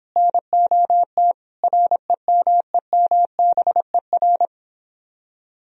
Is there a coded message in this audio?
Yes: Morse code from 0.2 to 4.5 s, heard as a keyed tone, about 24 wpm, 700 Hz. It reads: NOT REMEMBER